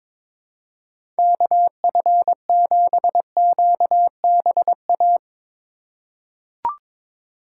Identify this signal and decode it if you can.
Morse code, heard as a keyed tone, about 22 wpm, 700 Hz. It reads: KF7QBA
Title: USA Call Signs - 6 Characters - 2x Character Spacing - 22wpm